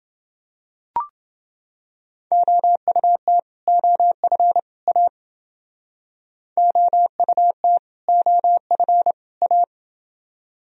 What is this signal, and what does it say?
Morse code, heard as a keyed tone, about 28 wpm, 700 Hz. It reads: OUT OF A OUT OF A